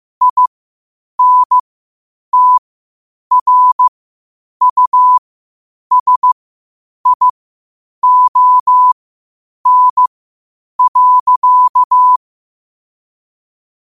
Morse code, heard as a keyed tone, about 15 wpm, spaced as for 8 wpm, 1000 Hz. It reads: INTRUSION.
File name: qotdfiles / 8wpm